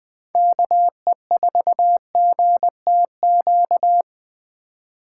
Morse code, heard as a keyed tone, about 21 wpm, 700 Hz. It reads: KE4GTQ
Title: 2023 POTA Activator Call Signs - 20wpm